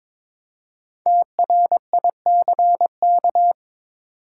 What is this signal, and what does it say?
Morse code, heard as a keyed tone, about 23 wpm, 700 Hz. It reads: TRICK
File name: 5 Letter Words - First 3 Letters Given - 22wpm